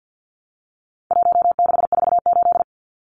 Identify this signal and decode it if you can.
Morse code, heard as a keyed tone, about 50 wpm, 700 Hz. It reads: 1648
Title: Numbers - 4-Digit 50wpm